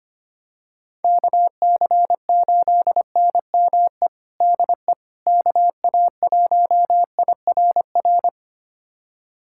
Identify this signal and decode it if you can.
Morse code, heard as a keyed tone, about 25 wpm, 700 Hz. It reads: KC8NME DE KA1IRR
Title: USA Callsign DE Callsign - 6 Characters - 25wpm